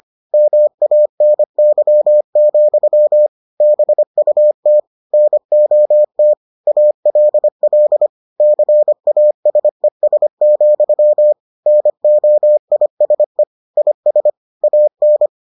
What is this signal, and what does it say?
Morse code, heard as a keyed tone, about 25 wpm, 600 Hz. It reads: MANY, BUT NOT ALL CASES, NOISE IS AN